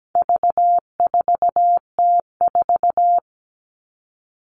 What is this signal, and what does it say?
Morse code, heard as a keyed tone, about 17 wpm, 700 Hz. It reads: V4T4
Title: All Cut Numbers - 4-Digit 17wpm